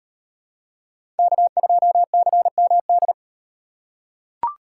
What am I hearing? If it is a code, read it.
Morse code, heard as a keyed tone, about 37 wpm, 700 Hz. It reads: K2CMD